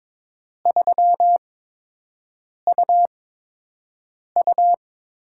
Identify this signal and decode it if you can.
Morse code, heard as a keyed tone, about 22 wpm, spaced as for 6 wpm, 700 Hz. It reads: 3UU
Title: USA Call Signs - 6 Characters - 8x Character Spacing - 22wpm